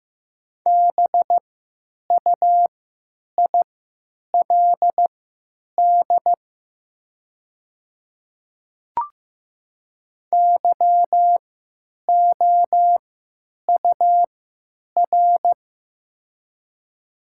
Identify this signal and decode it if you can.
Morse code, heard as a keyed tone, about 15 wpm, spaced as for 9 wpm, 700 Hz. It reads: BUILD YOUR
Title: 3x Character Spacing - Top 200 Words - 15wpm